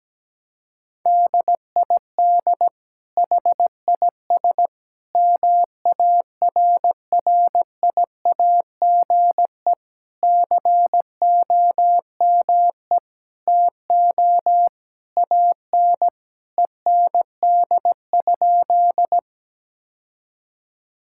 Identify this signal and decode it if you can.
Morse code, heard as a keyed tone, about 17 wpm, 700 Hz. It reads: DID HIS MARRIAGE COME TO AN END?
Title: Sentences from Top 1000 Words - Review Part 5 - 17wpm